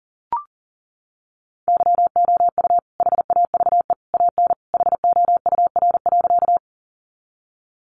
Morse code, heard as a keyed tone, about 40 wpm, 700 Hz. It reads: YOU HAVE AN HOUR.
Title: Sentences from Top 400 Words 40wpm